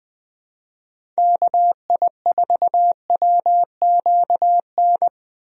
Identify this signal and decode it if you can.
Morse code, heard as a keyed tone, about 20 wpm, 700 Hz. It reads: KI4WQN
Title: USA Call Signs - 6 Characters - Repeated 2x - 20wpm